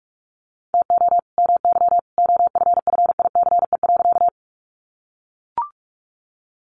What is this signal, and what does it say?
Morse code, heard as a keyed tone, about 44 wpm, 700 Hz. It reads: TO MY OFFICE.
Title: Sentences from Top 1000 Words 45wpm